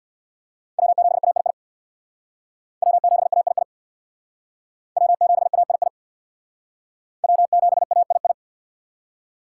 Morse code, heard as a keyed tone, about 61 wpm, 700 Hz. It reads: W7AII W7AII W7AII W7AII
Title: Speed Racing - USA Call Signs 50wpm